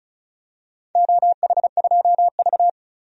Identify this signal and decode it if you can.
Morse code, heard as a keyed tone, about 36 wpm, 700 Hz. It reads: OH2V
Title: DXCC-100 Call Signs - 4 Characters - 35wpm